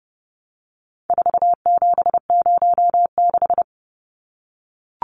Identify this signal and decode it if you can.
Morse code, heard as a keyed tone, about 30 wpm, 700 Hz. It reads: 4706
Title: Numbers - 4-Digit 30wpm